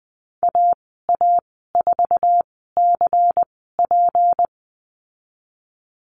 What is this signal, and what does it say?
Morse code, heard as a keyed tone, about 20 wpm, spaced as for 15 wpm, 700 Hz. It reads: AA4CP
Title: 173 Lesson 28g - Number 9 - Call Signs 20wpm